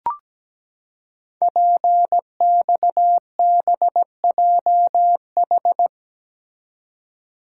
Morse code, heard as a keyed tone, about 17 wpm, 700 Hz. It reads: PXBJH
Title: Code Groups - 5 Letters - 17wpm